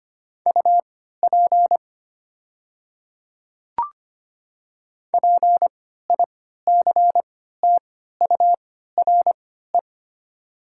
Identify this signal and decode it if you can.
Morse code, heard as a keyed tone, about 25 wpm, spaced as for 14 wpm, 700 Hz. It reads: UP PICTURE
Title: 3x Character Spacing - Top 300 Words - 25wpm